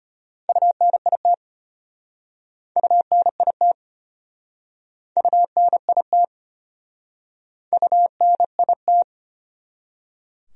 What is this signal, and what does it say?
Morse code, heard as a keyed tone, about 36 wpm, 700 Hz. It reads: UNIT UNIT UNIT UNIT